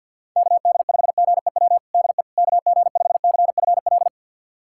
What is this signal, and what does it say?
Morse code, heard as a keyed tone, about 49 wpm, 700 Hz. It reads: KD5GEW DE KG5CFL
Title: USA Callsign DE Callsign - 6 Characters - 50wpm